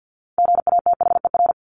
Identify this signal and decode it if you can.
Morse code, heard as a keyed tone, about 48 wpm, 700 Hz. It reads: GATHER
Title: Top 1501-1750 Words 50wpm